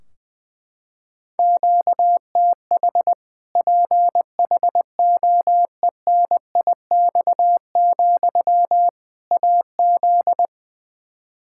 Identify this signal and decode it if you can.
Morse code, heard as a keyed tone, about 20 wpm, 700 Hz. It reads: QTH PHOENIX, AZ
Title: Practice QSO - Exchange 1 - Beginning - 20wpm